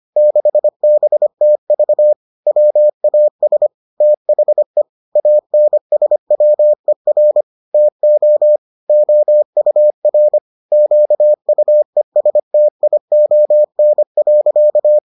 Morse code, heard as a keyed tone, about 25 wpm, 600 Hz. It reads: 6BTV WAS THE ANSWER TO OUR QUESTION.